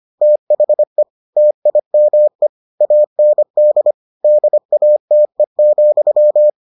Morse code, heard as a keyed tone, about 25 wpm, 600 Hz. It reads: THE TIME AND DATE,